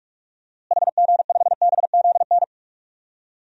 Morse code, heard as a keyed tone, about 42 wpm, 700 Hz. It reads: HG5BZN
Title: DXCC-100 Call Signs - 6 Characters - Repeated 3x - 45wpm